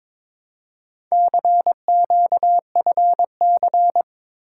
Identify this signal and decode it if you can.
Morse code, heard as a keyed tone, about 22 wpm, 700 Hz. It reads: CQFC
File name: practice / 22wpm / Four Letter-Number Combinations - 22wpm